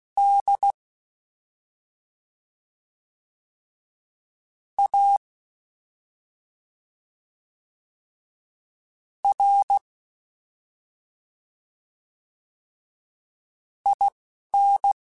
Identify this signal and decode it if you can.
Morse code, heard as a keyed tone, about 16 wpm, spaced as for 3 wpm, 800 Hz. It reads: DARIN
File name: Special INGs-mp3s / Special INGs-mp3s / Slow ing @10